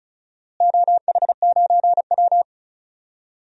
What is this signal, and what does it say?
Morse code, heard as a keyed tone, about 36 wpm, 700 Hz. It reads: OH9W
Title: DXCC-100 Call Signs - 4 Characters - Repeated 2x - 35wpm